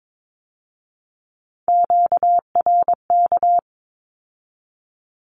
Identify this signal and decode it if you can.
Morse code, heard as a keyed tone, about 22 wpm, 700 Hz. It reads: QRK
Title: Q-Codes with Spoken Meaning - 22wpm